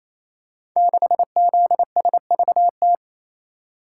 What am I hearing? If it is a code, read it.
Morse code, heard as a keyed tone, about 27 wpm, 700 Hz. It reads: 6ZSVT
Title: Code Groups - 5 Letter:Numbers - 28wpm